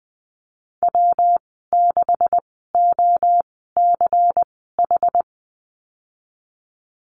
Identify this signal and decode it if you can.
Morse code, heard as a keyed tone, about 19 wpm, spaced as for 14 wpm, 700 Hz. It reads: W6OCH
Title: 186 Lesson 30g - Letter b - Call Signs 20wpm